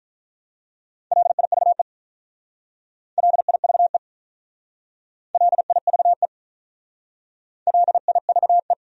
Morse code, heard as a keyed tone, about 44 wpm, 700 Hz. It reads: LIVE LIVE LIVE LIVE